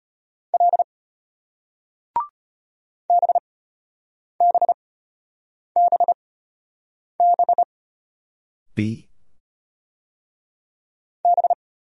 Morse code, heard as a keyed tone, about 33 wpm, spaced as for 8 wpm, 700 Hz. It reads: L BBBB B